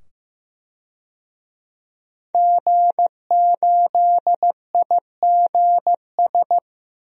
Morse code, heard as a keyed tone, about 15 wpm, 700 Hz. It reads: G8IGS
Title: DXCC-100 Call Signs - 5 Characters - 15wpm